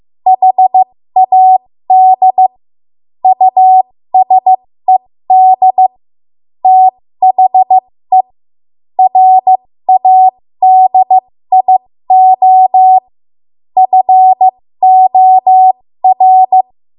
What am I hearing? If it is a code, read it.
Morse code, heard as a keyed tone, about 15 wpm, spaced as for 13 wpm, 750 Hz. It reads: HAD USED THE RADIO FOR
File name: Morse / 13WPM